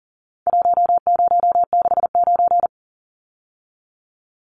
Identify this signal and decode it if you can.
Morse code, heard as a keyed tone, about 40 wpm, 700 Hz. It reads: 1069
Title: Numbers - Contest Style - 4-Digit 40wpm